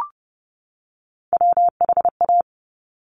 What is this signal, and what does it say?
Morse code, heard as a keyed tone, about 30 wpm, 700 Hz. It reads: WHA
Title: Top Three Letter Combinations in Words 30wpm